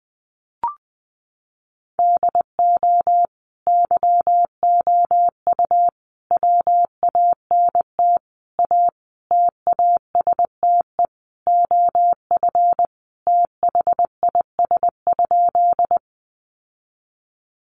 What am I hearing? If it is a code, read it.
Morse code, heard as a keyed tone, about 20 wpm, 700 Hz. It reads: DO YOU WANT A TASTE OF THIS?
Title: Sentences from Top 1100 Words 20wpm